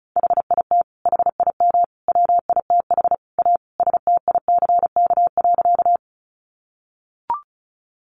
Morse code, heard as a keyed tone, about 34 wpm, 700 Hz. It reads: HIT HIM WITH A STICK.
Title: Sentences from Top 1200 Words 35wpm